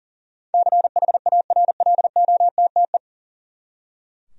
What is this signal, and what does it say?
Morse code, heard as a keyed tone, about 40 wpm, 700 Hz. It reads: CHARLOTTE
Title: USA Common Names - 40wpm